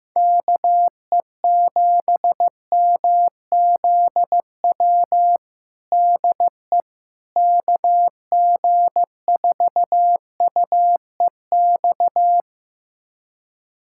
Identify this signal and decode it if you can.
Morse code, heard as a keyed tone, about 15 wpm, 700 Hz. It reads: KE7MZW DE KG4UEX
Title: USA Callsign DE Callsign - 6 Characters - 15wpm